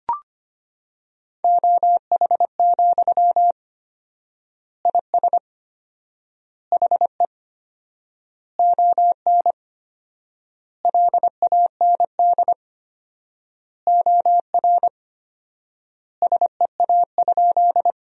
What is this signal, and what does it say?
Morse code, heard as a keyed tone, about 25 wpm, 700 Hz. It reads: OH, IS HE ON LAND OR SEA?
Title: 4x Word-Spacing - Sentences from Top 500 Words - 25wpm